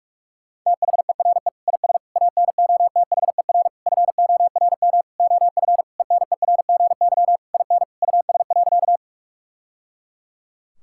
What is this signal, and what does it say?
Morse code, heard as a keyed tone, about 41 wpm, 700 Hz. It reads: THERE IS ANOTHER FORM OF ENERGY IN US.
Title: Sentences from Top 1000 Words - Review Part 6 - 45wpm